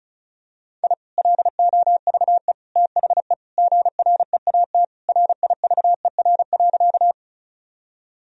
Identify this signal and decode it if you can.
Morse code, heard as a keyed tone, about 36 wpm, 700 Hz. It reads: I LOVE THE GREAT RIVER.
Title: Sentences from Top 1000 Words - Review Part 1 - 35wpm